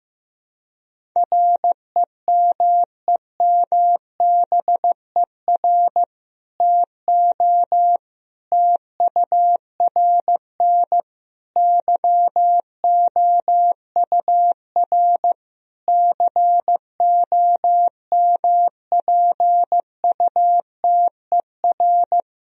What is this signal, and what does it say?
Morse code, heard as a keyed tone, about 15 wpm, 700 Hz. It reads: REMEMBER TO TURN YOUR COMPUTER
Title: Sentences from Top 1000 Words - Review Part 3 - 15wpm